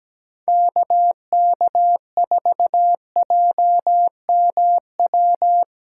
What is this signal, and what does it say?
Morse code, heard as a keyed tone, about 17 wpm, 700 Hz. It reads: KK4JMW
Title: USA Call Signs - 6 Characters - 6x Character Spacing - 17wpm